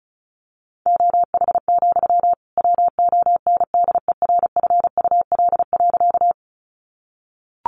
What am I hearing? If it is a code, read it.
Morse code, heard as a keyed tone, about 34 wpm, 700 Hz. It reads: OH, WONDERFUL.
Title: Sentences from Top 1100 Words 35wpm